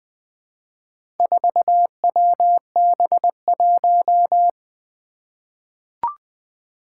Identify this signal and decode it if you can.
Morse code, heard as a keyed tone, about 20 wpm, 700 Hz. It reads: 4WB1
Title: Four Challenging Letter-Number Combinations 20wpm